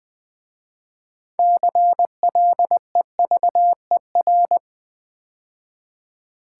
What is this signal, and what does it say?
Morse code, heard as a keyed tone, about 20 wpm, 700 Hz. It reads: CLEVER
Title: 6 Letter Words - First 3 Letters Given - 20wpm